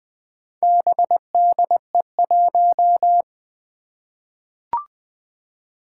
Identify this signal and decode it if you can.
Morse code, heard as a keyed tone, about 20 wpm, 700 Hz. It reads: BDE1
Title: Four Challenging Letter-Number Combinations 20wpm